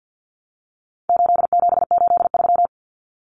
Morse code, heard as a keyed tone, about 50 wpm, 700 Hz. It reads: 8783
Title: Numbers - 4-Digit 50wpm